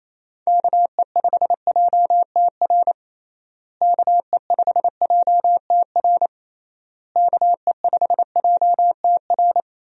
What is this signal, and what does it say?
Morse code, heard as a keyed tone, about 27 wpm, 700 Hz. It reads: KE5JTR KE5JTR KE5JTR
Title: USA Call Signs - 6 Characters - Repeated 3x - 28wpm